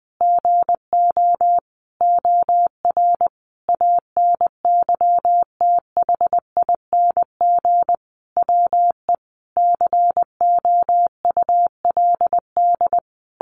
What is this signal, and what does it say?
Morse code, heard as a keyed tone, about 20 wpm, 700 Hz. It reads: GO OR ANYTHING WE COULD
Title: Sentences from Top 1400 Words 20wpm